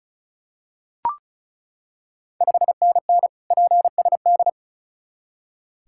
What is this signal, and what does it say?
Morse code, heard as a keyed tone, about 36 wpm, 690 Hz. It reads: 5NN PSD
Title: Ontario QSO Party Exchange 35wpm